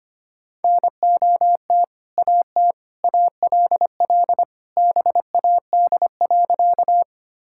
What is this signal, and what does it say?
Morse code, heard as a keyed tone, about 25 wpm, 700 Hz. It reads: NOT AT ALL BAD.
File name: practice / 25wpm / Sentences with 2-Word Phrases - 25wpm